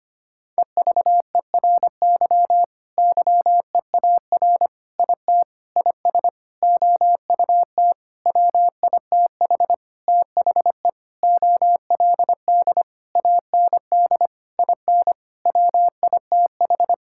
Morse code, heard as a keyed tone, about 25 wpm, 700 Hz. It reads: EVERY YEAR IT IS OUT WITH THE OLD AND IN WITH
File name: practice / 25wpm / Sentences with 9-Word Phrases - 25wpm